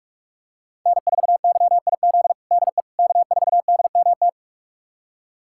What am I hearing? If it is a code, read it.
Morse code, heard as a keyed tone, about 46 wpm, 700 Hz. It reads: N4YIZ DE K4DMT
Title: USA Callsign DE Callsign - 5 Characters - 45wpm